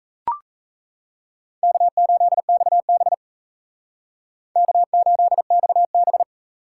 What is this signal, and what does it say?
Morse code, heard as a keyed tone, about 36 wpm, 700 Hz. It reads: K8XB K8XB